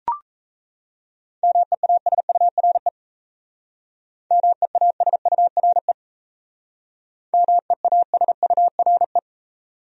Morse code, heard as a keyed tone, about 35 wpm, 700 Hz. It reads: MEASURE MEASURE MEASURE
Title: Speed-Racing Top 500 Words 28wpm